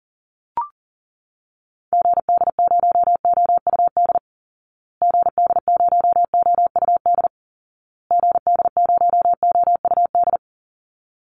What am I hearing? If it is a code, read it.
Morse code, heard as a keyed tone, about 37 wpm, 700 Hz. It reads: GD0OUD GD0OUD GD0OUD
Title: Call Signs Repeated 2x - 40wpm